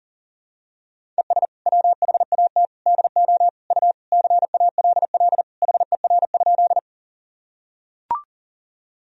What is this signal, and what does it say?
Morse code, heard as a keyed tone, about 37 wpm, 700 Hz. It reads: ES WHAT DO U CALL HER?
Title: Sentences from Top 100 Words and Abbreviations 40wpm